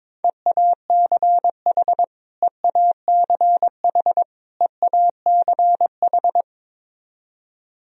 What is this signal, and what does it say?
Morse code, heard as a keyed tone, about 22 wpm, 700 Hz. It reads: EACH EACH EACH
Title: Top 100 Words Repeated 22wpm